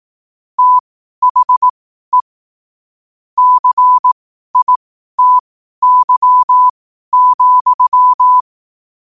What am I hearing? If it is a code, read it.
Morse code, heard as a keyed tone, about 18 wpm, spaced as for 12 wpm, 1000 Hz. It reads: THE CITY,
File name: qotdfiles / 12wpm